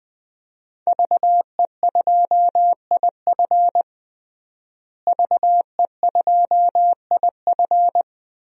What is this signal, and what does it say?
Morse code, heard as a keyed tone, about 21 wpm, 700 Hz. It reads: VE2IF VE2IF
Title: DXCC-100 Call Signs - 5 Characters - Repeated 3x - 20wpm